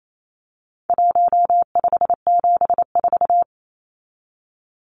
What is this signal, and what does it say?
Morse code, heard as a keyed tone, about 28 wpm, 700 Hz. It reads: 1574
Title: Numbers - 4-Digit 28wpm